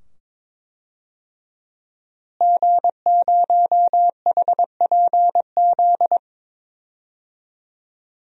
Morse code, heard as a keyed tone, about 22 wpm, 700 Hz. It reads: G0HPZ
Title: DXCC-100 Call Signs - 5 Characters - Repeated 2x - 22wpm